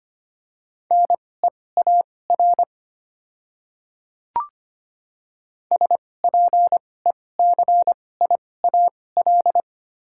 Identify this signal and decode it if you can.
Morse code, heard as a keyed tone, about 25 wpm, spaced as for 18 wpm, 700 Hz. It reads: NEAR SPECIAL
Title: 2x Character Spacing - Top 500 Words - 25wpm